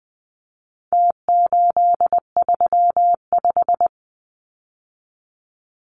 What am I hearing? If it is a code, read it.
Morse code, heard as a keyed tone, about 20 wpm, 700 Hz. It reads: T835
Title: Numbers - Contest Style - 4-Digit 20wpm